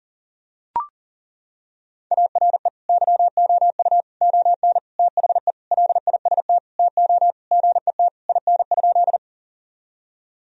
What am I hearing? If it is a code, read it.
Morse code, heard as a keyed tone, about 40 wpm, 700 Hz. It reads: ARE YOU ON THE LIST TO GET IN?
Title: Sentences from Top 1000 Words - Review Part 2 - 40wpm